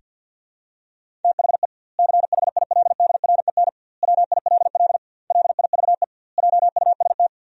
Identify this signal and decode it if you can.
Morse code, heard as a keyed tone, about 50 wpm, 700 Hz. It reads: THE CHILDREN WILL LIVE JUST
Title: 2x Word-Spacing - Sentences from Top 300 Words - 50wpm